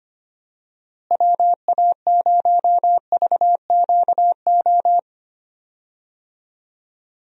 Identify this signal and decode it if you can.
Morse code, heard as a keyed tone, about 25 wpm, 700 Hz. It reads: WA0VQO
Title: USA Call Signs - 6 Characters - Repeated 3x - 25wpm